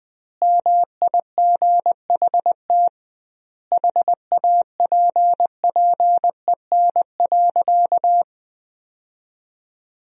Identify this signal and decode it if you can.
Morse code, heard as a keyed tone, about 21 wpm, 700 Hz. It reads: MIGHT HAPPEN.
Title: 2x Word-Spacing - Sentences from Top 400 Words - 20wpm